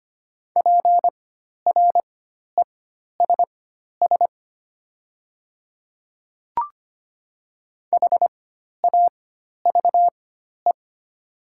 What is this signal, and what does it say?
Morse code, heard as a keyed tone, about 25 wpm, spaced as for 12 wpm, 700 Hz. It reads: PRESS HAVE
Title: 4x Character Spacing - Top 500 Words - 25wpm